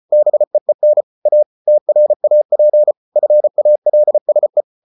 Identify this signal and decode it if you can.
Morse code, heard as a keyed tone, about 34 wpm, 600 Hz. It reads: BEEN A TRAP FALSE